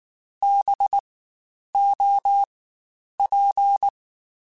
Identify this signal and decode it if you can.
Morse code, heard as a keyed tone, about 19 wpm, spaced as for 9 wpm, 800 Hz. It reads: BOP
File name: Spaced Alphas / Spaced Alphas mp3s / Bs sp @18